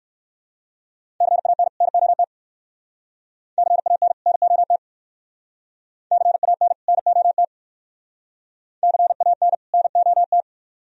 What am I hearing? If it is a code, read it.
Morse code, heard as a keyed tone, about 55 wpm, 700 Hz. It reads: CAN NOT CAN NOT CAN NOT CAN NOT
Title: Speed-Racing - Sets of 2 Words 45wpm